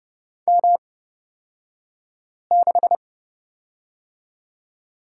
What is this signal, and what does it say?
Morse code, heard as a keyed tone, about 30 wpm, spaced as for 5 wpm, 700 Hz. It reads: M6
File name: Lesson 37d - Letter x - Two Characters 30wpm